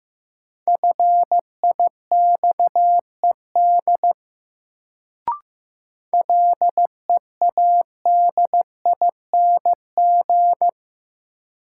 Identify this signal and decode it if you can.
Morse code, heard as a keyed tone, about 16 wpm, 700 Hz. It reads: FIXED LEADING
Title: Sentences from Top 1100 Words - New Word Drill - 15wpm